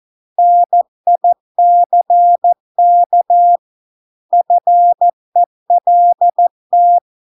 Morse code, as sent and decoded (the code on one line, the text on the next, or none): -. .. -.-. -.- / ..-. .-.. -
NICK FELT